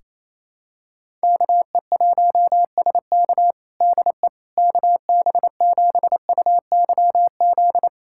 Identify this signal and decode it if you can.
Morse code, heard as a keyed tone, about 28 wpm, 700 Hz. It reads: KE1SK DE KB7UYZ
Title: USA Callsign DE Callsign 28wpm